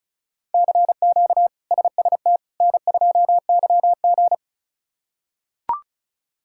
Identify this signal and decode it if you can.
Morse code, heard as a keyed tone, about 36 wpm, 700 Hz. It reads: CQ SST N2YG
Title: K1USN SST QSO Practice - 35wpm